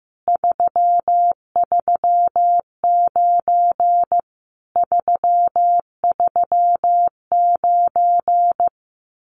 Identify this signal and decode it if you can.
Morse code, heard as a keyed tone, about 15 wpm, 700 Hz. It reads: 339 339
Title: Common QSO Phrases Encore 15wpm